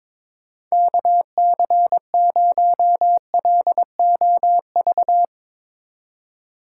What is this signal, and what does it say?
Morse code, heard as a keyed tone, about 22 wpm, 700 Hz. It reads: KC0LOV
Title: USA Call Signs - 6 Characters - 3x Character Spacing - 22wpm